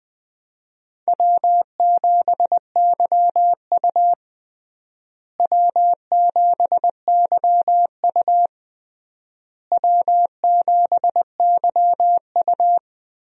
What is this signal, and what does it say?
Morse code, heard as a keyed tone, about 19 wpm, 700 Hz. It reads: W7YU W7YU W7YU
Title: USA Call Signs - 4 Characters - Repeated 3x - 20wpm